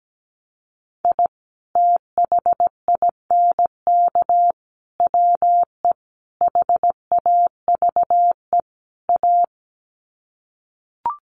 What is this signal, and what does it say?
Morse code, heard as a keyed tone, about 17 wpm, 700 Hz. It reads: I THINK WE HAVE A
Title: Sets of 5 Words Encore 17wpm